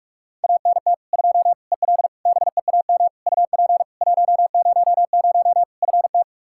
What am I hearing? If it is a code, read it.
Morse code, heard as a keyed tone, about 46 wpm, 700 Hz. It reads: ANT 2 EL BEAM UP 100 FT
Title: Practice QSO - Exchange 2 - Info - 45wpm